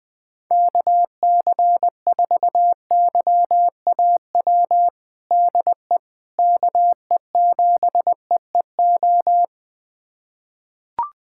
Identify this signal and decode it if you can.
Morse code, heard as a keyed tone, about 21 wpm, 700 Hz. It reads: KC4YAW DE KE7EEO
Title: USA Callsign DE Callsign 20wpm